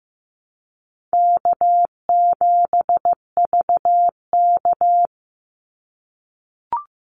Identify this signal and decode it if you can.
Morse code, heard as a keyed tone, about 15 wpm, 700 Hz. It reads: K7VK
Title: NA SOTA Activator Callsigns 15wpm